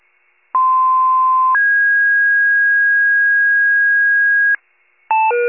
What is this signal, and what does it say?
Two-tone page, heard as a one-second tone, then a three-second tone, about 1040 then 1650 Hz. It is heard from 0.5 to 4.6 s.
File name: pager / 95 RESCUE DUTY / wcalert